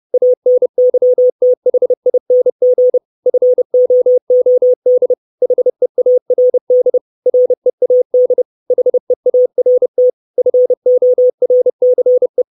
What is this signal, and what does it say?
Morse code, heard as a keyed tone, about 30 wpm, 500 Hz. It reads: ANYTHING FOOD HEARD READ HEART FORCE
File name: 632 Words 30WPM